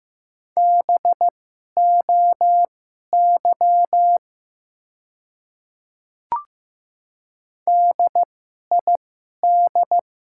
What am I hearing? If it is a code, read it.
Morse code, heard as a keyed tone, about 15 wpm, spaced as for 11 wpm, 700 Hz. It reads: BOY DID
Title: 2x Character Spacing - Top 200 Words - 15wpm